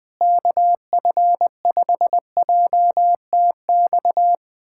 Morse code, heard as a keyed tone, about 20 wpm, 700 Hz. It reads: KF5JTX
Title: USA Callsign DE Callsign - 6 Characters - 20wpm